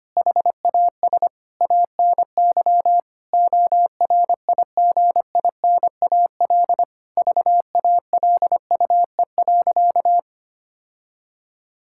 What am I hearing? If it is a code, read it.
Morse code, heard as a keyed tone, about 25 wpm, 700 Hz. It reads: HAS ANY ORIGINAL VALUE.
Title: Sentences from Top 1000 Words - Review Part 6 - 25wpm